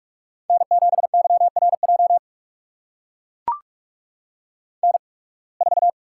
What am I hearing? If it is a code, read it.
Morse code, heard as a keyed tone, about 44 wpm, spaced as for 29 wpm, 700 Hz. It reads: N7YRJ N4
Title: USA Call Signs - 5 Characters - 8x Character Spacing - 45wpm